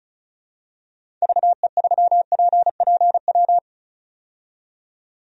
Morse code, heard as a keyed tone, about 36 wpm, 700 Hz. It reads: VE3PPW
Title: DXCC-100 Call Signs - 6 Characters - 35wpm